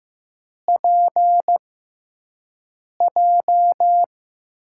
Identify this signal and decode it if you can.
Morse code, heard as a keyed tone, about 16 wpm, spaced as for 5 wpm, 700 Hz. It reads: PJ
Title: USA Call Signs - 6 Characters - 6x Character Spacing - 15wpm